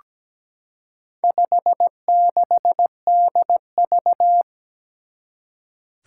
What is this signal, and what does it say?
Morse code, heard as a keyed tone, about 17 wpm, 700 Hz. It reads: 56DV
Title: Four Challenging Letter-Number Combinations 17wpm